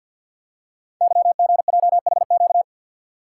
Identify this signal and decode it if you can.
Morse code, heard as a keyed tone, about 50 wpm, 700 Hz. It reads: YGJHQ
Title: Code Groups - 5 Letters - 50wpm